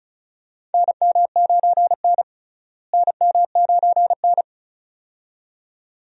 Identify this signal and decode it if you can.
Morse code, heard as a keyed tone, about 34 wpm, 700 Hz. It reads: NM9N NM9N
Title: USA Call Signs - 4 Characters - Repeated 2x - 35wpm